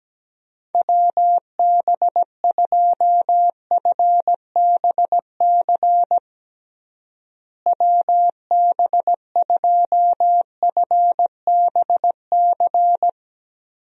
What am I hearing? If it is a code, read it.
Morse code, heard as a keyed tone, about 17 wpm, 700 Hz. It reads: WB2FBC WB2FBC
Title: USA Call Signs - 6 Characters - Repeated 3x - 17wpm